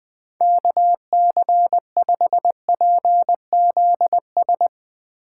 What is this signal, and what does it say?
Morse code, heard as a keyed tone, about 22 wpm, 700 Hz. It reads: KC5PZS
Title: Speed Racing - USA 2x3 Call Signs 20wpm